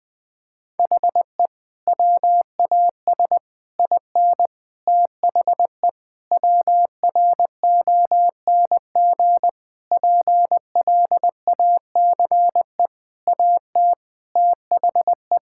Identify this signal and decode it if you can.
Morse code, heard as a keyed tone, about 20 wpm, 700 Hz. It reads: HE WAS IN THE WRONG PLACE AT THE
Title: Sentences with 8-Word Phrases - 20wpm